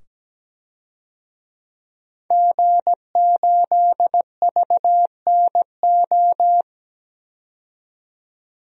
Morse code, heard as a keyed tone, about 17 wpm, 700 Hz. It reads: G8VNO
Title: DXCC-100 Call Signs - 5 Characters - 17wpm